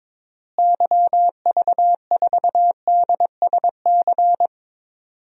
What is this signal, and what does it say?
Morse code, heard as a keyed tone, about 23 wpm, 700 Hz. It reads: YV4DSC
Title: DXCC-100 Call Signs - 6 Characters - Repeated 2x - 22wpm